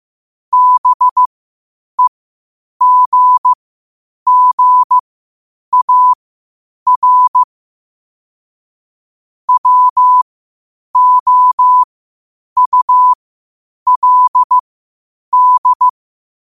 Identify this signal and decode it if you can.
Morse code, heard as a keyed tone, about 15 wpm, spaced as for 8 wpm, 1000 Hz. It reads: BEGGAR WOULD
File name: qotdfiles / 8wpm